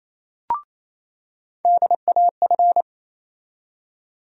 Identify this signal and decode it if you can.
Morse code, heard as a keyed tone, about 27 wpm, 700 Hz. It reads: DAF
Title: USA Call Sign Postfixes - 3 Letters 28wpm